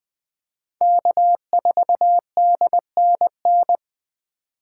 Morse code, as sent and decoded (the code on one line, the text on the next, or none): -.- ....- -.. -. -.
K4DNN